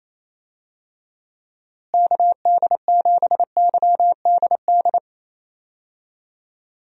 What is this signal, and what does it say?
Morse code, heard as a keyed tone, about 28 wpm, 700 Hz. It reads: KD7YDD
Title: USA Call Signs - 6 Characters - Repeated 2x - 28wpm